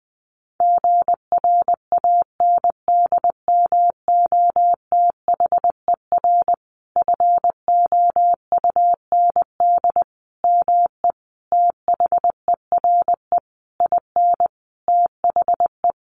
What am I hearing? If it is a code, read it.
Morse code, heard as a keyed tone, about 20 wpm, 700 Hz. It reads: GRANDMOTHER FOUND ME THERE IN THE
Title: Sentences from Top 1200 Words 20wpm